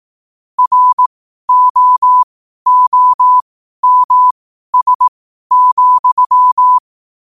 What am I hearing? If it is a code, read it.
Morse code, heard as a keyed tone, about 18 wpm, spaced as for 12 wpm, 1000 Hz. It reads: ROOMS,